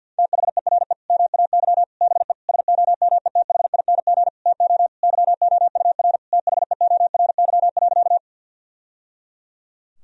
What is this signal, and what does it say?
Morse code, heard as a keyed tone, about 52 wpm, 700 Hz. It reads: THERE MAY BE SOMETHING TO YOUR THEORY.
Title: Sentences from Top 1000 Words - Review Part 3 - 50wpm